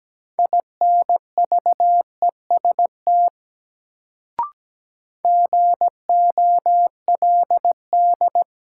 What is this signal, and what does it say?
Morse code, heard as a keyed tone, about 17 wpm, 700 Hz. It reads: INVEST GOLD
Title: Sentences from Top 1100 Words - New Word Drill Alt - 17wpm